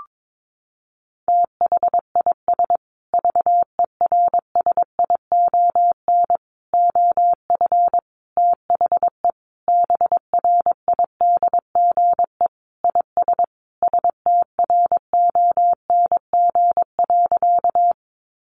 Morse code, heard as a keyed tone, about 22 wpm, 700 Hz. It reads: THIS VERSION OF THE BRIDGE IS STRONG.
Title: Sentences from Top 1400 Words 22wpm